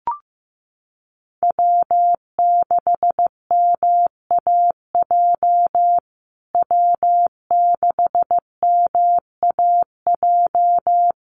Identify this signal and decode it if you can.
Morse code, heard as a keyed tone, about 15 wpm, 700 Hz. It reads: W6MAJ W6MAJ
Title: USA Call Signs Repeated 3x 15wpm